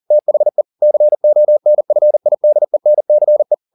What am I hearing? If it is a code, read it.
Morse code, heard as a keyed tone, about 40 wpm, 600 Hz. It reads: THE CONFIDENCE
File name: the old man and the sea 2 40wpm